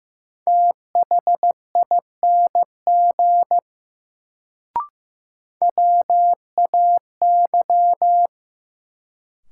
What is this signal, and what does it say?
Morse code, heard as a keyed tone, about 15 wpm, 700 Hz. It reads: THING WAY
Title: Sentences from Top 100 Words - New Word Drill - 15wpm